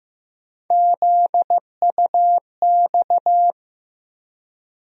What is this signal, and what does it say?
Morse code, heard as a keyed tone, about 15 wpm, 700 Hz. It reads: ZUX
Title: USA Call Sign Postfixes - 3 Letters 15wpm